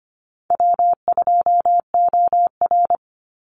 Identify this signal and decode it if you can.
Morse code, heard as a keyed tone, about 25 wpm, 700 Hz. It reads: W2OR
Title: Easy Call Signs - 25wpm